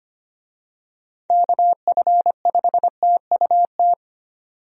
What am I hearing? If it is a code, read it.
Morse code, heard as a keyed tone, about 25 wpm, 700 Hz. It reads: KF5TUT
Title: USA Call Signs - 6 Characters - 25wpm